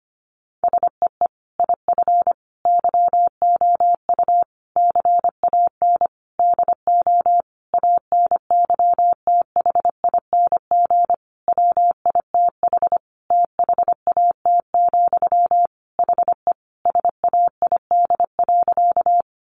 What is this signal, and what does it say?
Morse code, heard as a keyed tone, about 25 wpm, 700 Hz. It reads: SEE IF YOU CAN DO ANYTHING WITH THAT, HE SAID.
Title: Sentences from Top 1000 Words 25wpm